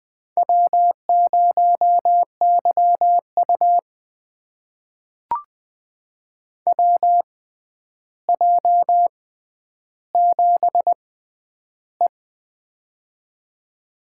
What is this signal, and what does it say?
Morse code, heard as a keyed tone, about 20 wpm, spaced as for 10 wpm, 700 Hz. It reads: W0YU WJ7E
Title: USA Call Signs - 4 Characters - 6x Character Spacing - 20wpm